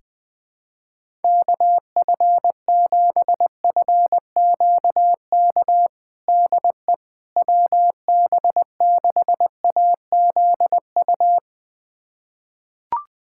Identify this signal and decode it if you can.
Morse code, heard as a keyed tone, about 20 wpm, 700 Hz. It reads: KF7FQK DE WB6AZU
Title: USA Callsign DE Callsign - 6 Characters - 20wpm